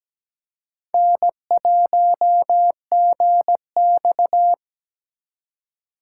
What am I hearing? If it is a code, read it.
Morse code, heard as a keyed tone, about 17 wpm, 700 Hz. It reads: N1GX
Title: USA Call Signs - 3x Character Spacing - 17wpm